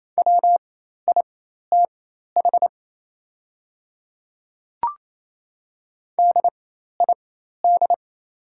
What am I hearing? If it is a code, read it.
Morse code, heard as a keyed tone, about 28 wpm, spaced as for 13 wpm, 700 Hz. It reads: WITH DID
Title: 4x Character Spacing - Top 100 Words - 28wpm